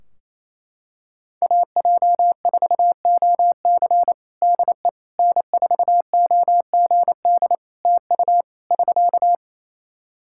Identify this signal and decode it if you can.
Morse code, heard as a keyed tone, about 28 wpm, 700 Hz. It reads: AJ4OC DE N4OGD TU <SK>
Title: Practice QSO - Exchange 4 - Ending - 28wpm